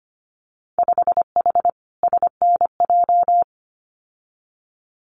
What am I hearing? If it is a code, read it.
Morse code, heard as a keyed tone, about 25 wpm, 690 Hz. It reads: 5H SNJ
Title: Winter Field Day Exchanges Encore 25wpm